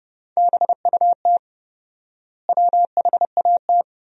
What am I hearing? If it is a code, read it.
Morse code, heard as a keyed tone, about 30 wpm, 700 Hz. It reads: BUT WHAT
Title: 4x Word-Spacing - Sentences from Top 100 Words - 30wpm